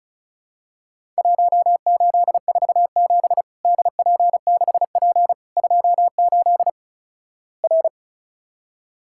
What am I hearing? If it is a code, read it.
Morse code, heard as a keyed tone, about 36 wpm, 700 Hz. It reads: 1847 DP6P 28 R
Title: Worked All Europe DX Contest - QTC Exchange - 35wpm